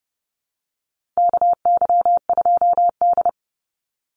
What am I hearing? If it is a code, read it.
Morse code, heard as a keyed tone, about 30 wpm, 700 Hz. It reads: KY2D
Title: Call Signs - 30wpm